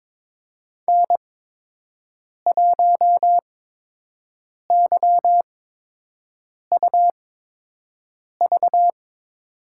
Morse code, heard as a keyed tone, about 22 wpm, spaced as for 6 wpm, 700 Hz. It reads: N1YUV
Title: USA Call Signs - 5 Characters - 8x Character Spacing - 22wpm